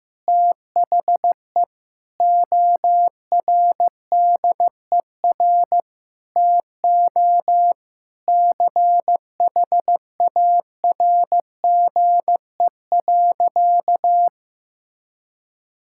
Morse code, heard as a keyed tone, about 15 wpm, 700 Hz. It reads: THE ORDER TO CHARGE.
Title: Sentences from Top 1000 Words - Review Part 6 - 15wpm